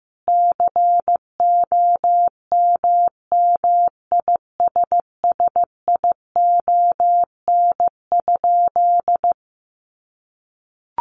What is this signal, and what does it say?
Morse code, heard as a keyed tone, about 15 wpm, 700 Hz. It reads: COMMISSION?
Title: Sentences from Top 1400 Words 15wpm